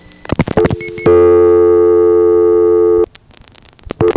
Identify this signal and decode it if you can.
Dial tone (North America), 350 + 440 Hz.